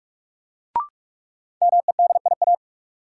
Morse code, heard as a keyed tone, about 46 wpm, 700 Hz. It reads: MEDIA
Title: Sentences from Top 1000 Words - Review Part 4 - New Word Drill - 45wpm